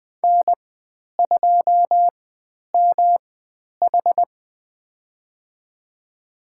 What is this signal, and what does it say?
Morse code, heard as a keyed tone, about 20 wpm, spaced as for 10 wpm, 700 Hz. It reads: N2MH
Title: 109 Lesson 18g - Letter m - Call Signs 20wpm